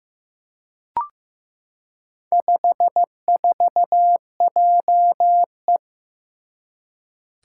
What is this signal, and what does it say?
Morse code, heard as a keyed tone, about 15 wpm, 700 Hz. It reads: 54JE
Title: Four Challenging Letter-Number Combinations 15wpm